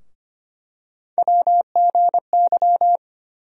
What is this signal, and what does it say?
Morse code, heard as a keyed tone, about 25 wpm, 700 Hz. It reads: WGY
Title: USA Call Sign Postfixes - 3 Letters 25wpm